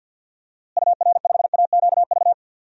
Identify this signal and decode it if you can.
Morse code, heard as a keyed tone, about 53 wpm, 700 Hz. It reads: UR5AQV
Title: DXCC-100 Call Signs - 6 Characters - Repeated 2x - 50wpm